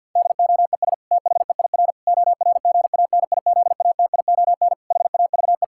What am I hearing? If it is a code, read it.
Morse code, heard as a keyed tone, about 52 wpm, 700 Hz. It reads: DOES THEIR ORGANIZATION HAVE
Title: Sentences from Top 1000 Words - Review Part 5 - 50wpm